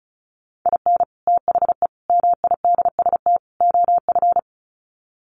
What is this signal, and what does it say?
Morse code, heard as a keyed tone, about 34 wpm, 700 Hz. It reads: IN THE MIDST OF
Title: Sets of 4 Words 35wpm